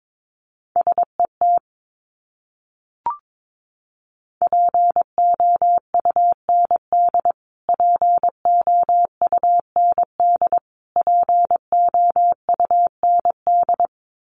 Morse code, heard as a keyed tone, about 22 wpm, 700 Hz. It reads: SET POUND POUND POUND
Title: Top 500 Words Repeated 22wpm